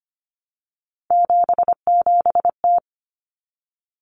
Morse code, heard as a keyed tone, about 25 wpm, 700 Hz. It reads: 77T
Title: Common Cut Numbers - 3-Digit 25wpm